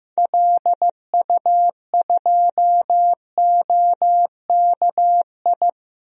Morse code, heard as a keyed tone, about 15 wpm, 700 Hz. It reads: LU2OKI
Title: DXCC-100 Call Signs - 6 Characters - Repeated 3x - 15wpm